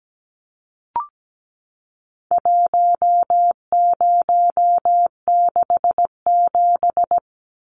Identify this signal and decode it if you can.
Morse code, heard as a keyed tone, about 17 wpm, 700 Hz. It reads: 1067
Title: Numbers - 4-Digit 17wpm